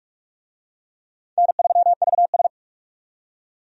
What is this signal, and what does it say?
Morse code, heard as a keyed tone, about 44 wpm, 700 Hz. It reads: N3VS
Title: USA Call Signs - 4 Characters - 45wpm